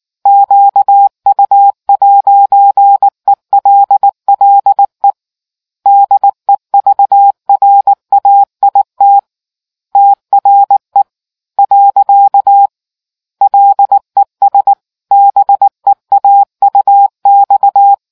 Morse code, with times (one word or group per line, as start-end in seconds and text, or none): 0.3-5.1 QU'ELLE
5.9-9.2 DEVRAIT
10.0-11.0 TRE
11.6-12.7 .
13.4-14.7 LES
15.1-17.9 BEAUX